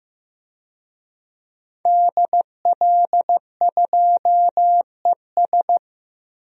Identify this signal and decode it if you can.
Morse code, heard as a keyed tone, about 16 wpm, 700 Hz. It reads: DL2ES
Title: DXCC-100 Call Signs - 5 Characters - Repeated 2x - 15wpm